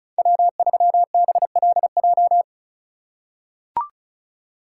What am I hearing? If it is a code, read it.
Morse code, heard as a keyed tone, about 36 wpm, 700 Hz. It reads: W3BLJ
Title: USA Call Signs - 5 Characters - 4x Character Spacing - 35wpm